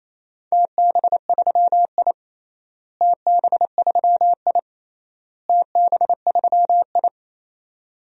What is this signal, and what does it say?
Morse code, heard as a keyed tone, about 28 wpm, 700 Hz. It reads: TB3I TB3I TB3I
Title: DXCC-100 Call Signs - 4 Characters - Repeated 3x - 28wpm